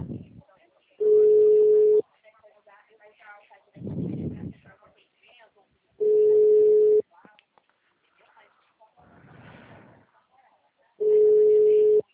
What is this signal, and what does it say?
Ringback tone (Europe, 425 Hz), with speech.